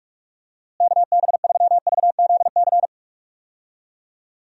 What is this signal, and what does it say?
Morse code, heard as a keyed tone, about 46 wpm, 700 Hz. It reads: KB3VZC
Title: USA Call Signs - 6 Characters - Repeated 3x - 45wpm